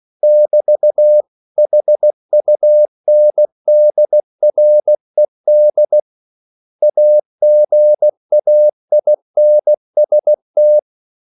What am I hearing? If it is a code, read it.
Morse code, heard as a keyed tone, about 16 wpm, 600 Hz. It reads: <BT>HUNDRED AGAINST
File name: CW / Practice / 500MostCommonEnglishWords16